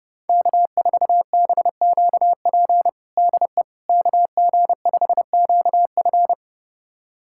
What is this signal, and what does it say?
Morse code, heard as a keyed tone, about 30 wpm, 690 Hz. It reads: K4BQP DE KG5QF
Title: USA Callsign DE Callsign 30wpm